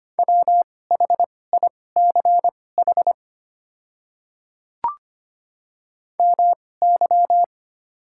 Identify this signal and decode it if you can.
Morse code, heard as a keyed tone, about 25 wpm, spaced as for 18 wpm, 700 Hz. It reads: WHICH MY
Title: 2x Character Spacing - Top 100 Words - 25wpm